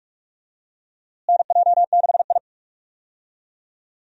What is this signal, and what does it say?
Morse code, heard as a keyed tone, about 42 wpm, 700 Hz. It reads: NJ6I
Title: USA Call Signs - 4 Characters - Repeated 2x - 45wpm